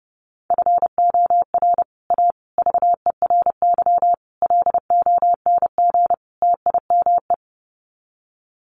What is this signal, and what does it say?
Morse code, heard as a keyed tone, about 29 wpm, 700 Hz. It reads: FOR A VERY LONG TIME
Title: Sets of 5 Words 30wpm